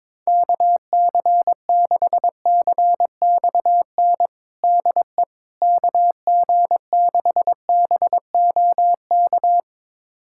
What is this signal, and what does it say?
Morse code, heard as a keyed tone, about 21 wpm, 700 Hz. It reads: KC6CXN DE KG6BOK
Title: USA Callsign DE Callsign - 6 Characters - 22wpm